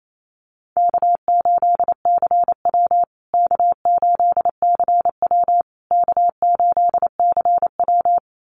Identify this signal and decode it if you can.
Morse code, heard as a keyed tone, about 28 wpm, 700 Hz. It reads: K8CW K8CW K8CW
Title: USA Call Signs Repeated 3x 28wpm